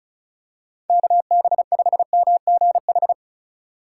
Morse code, heard as a keyed tone, about 34 wpm, 700 Hz. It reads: KB5MGH
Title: USA Call Signs - 3x Character Spacing - 35wpm